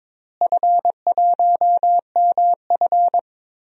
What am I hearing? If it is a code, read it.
Morse code, heard as a keyed tone, about 22 wpm, 700 Hz. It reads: F1MF
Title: DXCC-100 Call Signs - 4 Characters - Repeated 2x - 22wpm